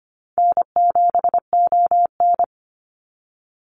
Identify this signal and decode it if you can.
Morse code, heard as a keyed tone, about 25 wpm, 700 Hz. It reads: N7ON
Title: Call Signs Repeated 3x - 25wpm